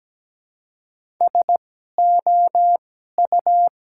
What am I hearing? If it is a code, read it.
Morse code, heard as a keyed tone, about 17 wpm, spaced as for 12 wpm, 700 Hz. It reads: SOU
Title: 2x Character Spacing - Top 200 Words - 17wpm